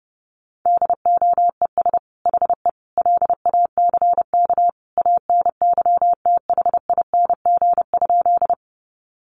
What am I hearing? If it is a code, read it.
Morse code, heard as a keyed tone, about 30 wpm, 700 Hz. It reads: DOES HE LACK ANYTHING?
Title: Sentences from Top 1100 Words 30wpm